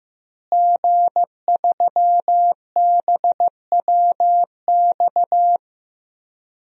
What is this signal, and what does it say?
Morse code, heard as a keyed tone, about 15 wpm, 700 Hz. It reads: G3BWX
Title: DXCC-100 Call Signs - 5 Characters - Repeated 3x - 15wpm